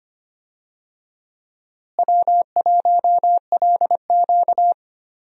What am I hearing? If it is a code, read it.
Morse code, heard as a keyed tone, about 25 wpm, 700 Hz. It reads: W1LQ
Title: USA Call Signs - 4 Characters - Repeated 3x - 25wpm